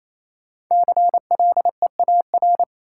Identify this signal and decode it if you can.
Morse code, heard as a keyed tone, about 28 wpm, 700 Hz. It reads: CLEAR
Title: Sets of 5 Words Encore - New Word Drill - 28wpm